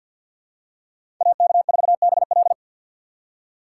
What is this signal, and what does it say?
Morse code, heard as a keyed tone, about 52 wpm, 700 Hz. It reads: AK4BL
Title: USA Call Signs - 5 Characters - 2x Character Spacing - 50wpm